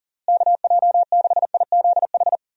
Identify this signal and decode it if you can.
Morse code, heard as a keyed tone, about 40 wpm, 700 Hz. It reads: KJ6IZH
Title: USA Call Signs - 6 Characters - Repeated 2x - 40wpm